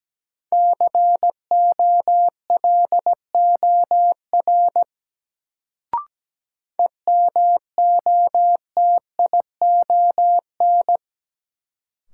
Morse code, heard as a keyed tone, about 17 wpm, 700 Hz. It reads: COLOR EMOTION
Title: Sentences from Top 1600 Words - New Word Drill Alt - 17wpm